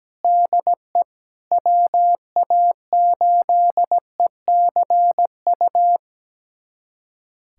Morse code, heard as a keyed tone, about 17 wpm, 700 Hz. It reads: DE WA8ECU
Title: USA Callsign DE Callsign - 6 Characters - 17wpm